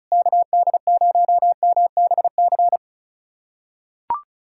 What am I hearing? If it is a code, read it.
Morse code, heard as a keyed tone, about 36 wpm, 700 Hz. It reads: KD0MBC
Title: USA Callsign DE Callsign - 6 Characters - 35wpm